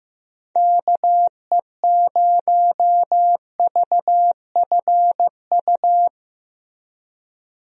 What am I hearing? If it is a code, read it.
Morse code, heard as a keyed tone, about 15 wpm, 700 Hz. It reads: KE0VFU
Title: USA Call Signs - 6 Characters - Repeated 2x - 15wpm